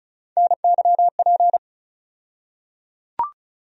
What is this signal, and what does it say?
Morse code, heard as a keyed tone, about 35 wpm, 700 Hz. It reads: NYP
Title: USA Call Sign Postfixes - 3 Letters 35wpm